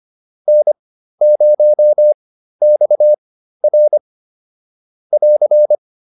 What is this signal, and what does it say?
Morse code, heard as a keyed tone, about 25 wpm, spaced as for 13 wpm, 600 Hz. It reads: N0XR <AR>